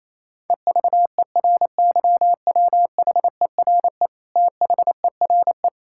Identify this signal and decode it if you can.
Morse code, heard as a keyed tone, about 28 wpm, 700 Hz. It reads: EVERYWHERE THERE
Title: Sentences from Top 1600 Words 28wpm